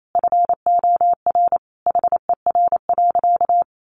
Morse code, heard as a keyed tone, about 28 wpm, 690 Hz. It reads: FOR HER.
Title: Sentences from Top 900 Words 28wpm